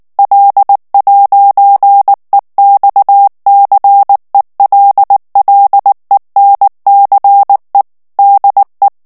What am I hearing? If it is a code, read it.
Morse code, heard as a keyed tone, about 19 wpm, 800 Hz. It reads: L'EXCELLENCE DE
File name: CW / FICHIERS-MP3 / FRANCAIS TEXTE CLAIR 2 VITESSE 18 MOTS MINUTES